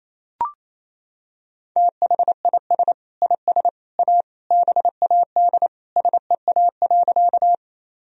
Morse code, heard as a keyed tone, about 28 wpm, 700 Hz. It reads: THIS IS A BAD SEA.
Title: Sentences from Top 1000 Words - Review Part 1 - 28wpm